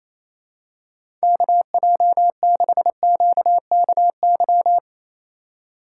Morse code, heard as a keyed tone, about 28 wpm, 700 Hz. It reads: KJ6QKY
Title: USA Call Signs - 6 Characters - 4x Character Spacing - 28wpm